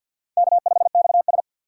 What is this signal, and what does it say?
Morse code, heard as a keyed tone, about 48 wpm, 700 Hz. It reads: K5XS
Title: USA Call Signs - 4x Character Spacing - 50wpm